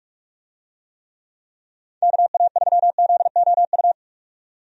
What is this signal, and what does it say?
Morse code, heard as a keyed tone, about 46 wpm, 700 Hz. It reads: KA3ZOU
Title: USA Call Signs - 6 Characters - 45wpm